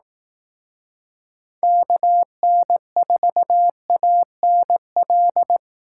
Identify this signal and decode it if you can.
Morse code, heard as a keyed tone, about 18 wpm, 700 Hz. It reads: KN4ANL
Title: Speed Racing - USA 2x3 Call Signs 15wpm